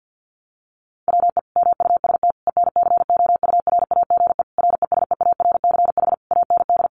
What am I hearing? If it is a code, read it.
Morse code, heard as a keyed tone, about 48 wpm, 700 Hz. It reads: WE MUST ENCOURAGE RESEARCH AND
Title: Sentences from Top 1400 Words 50wpm